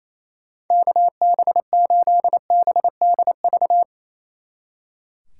Morse code, heard as a keyed tone, about 36 wpm, 700 Hz. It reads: KB8BDV